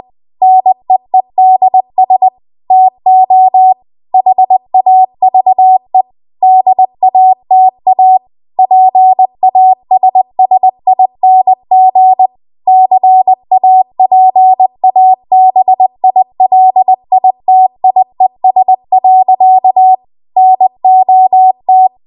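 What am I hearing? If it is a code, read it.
Morse code, heard as a keyed tone, about 20 wpm, 750 Hz. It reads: NEEDS TO HAVE DATA PASSING CAPABILITIES. NOT
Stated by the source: Morse code practice at 20 WPM.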